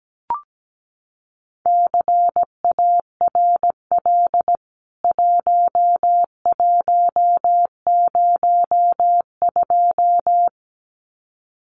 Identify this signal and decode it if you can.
Morse code, heard as a keyed tone, about 17 wpm, 700 Hz. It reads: CARL 1102
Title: CWT Exchange 17wpm